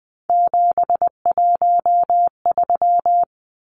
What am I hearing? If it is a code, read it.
Morse code, heard as a keyed tone, about 20 wpm, 700 Hz. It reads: 713
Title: Numbers - 3-Digit 20wpm